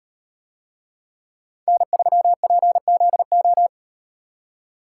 Morse code, heard as a keyed tone, about 38 wpm, 700 Hz. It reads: N3PZO